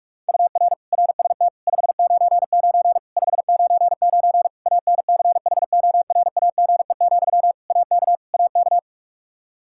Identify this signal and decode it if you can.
Morse code, heard as a keyed tone, about 41 wpm, 700 Hz. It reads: UR RST 599 599 ANCHORAGE, AK AK
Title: Practice QSO - Exchange 1 - Beginning - 45wpm